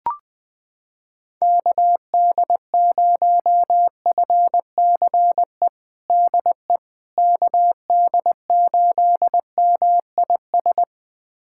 Morse code, heard as a keyed tone, about 20 wpm, 700 Hz. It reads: KD0FCE DE KD8MIS
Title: USA Callsign DE Callsign 20wpm